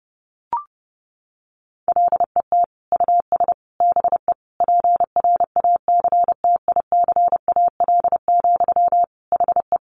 Morse code, heard as a keyed tone, about 30 wpm, 700 Hz. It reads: LET US BE PRACTICAL, HE
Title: Sentences from Top 1400 Words 30wpm